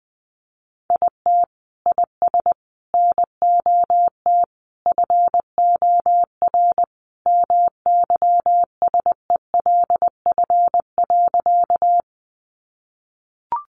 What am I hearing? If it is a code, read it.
Morse code, heard as a keyed tone, about 20 wpm, 700 Hz. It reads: IT IS NOT FOR MYSELF.
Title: Sentences from Top 1300 Words 20wpm